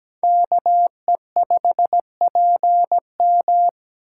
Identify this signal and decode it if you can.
Morse code, heard as a keyed tone, about 20 wpm, 700 Hz. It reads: KE5PM
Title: Speed Racing - USA 2x2 Call Signs 17wpm